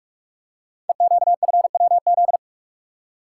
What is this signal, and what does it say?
Morse code, heard as a keyed tone, about 44 wpm, 700 Hz. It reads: EQFWZ
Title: Code Groups - 5 Letter:Numbers - 45wpm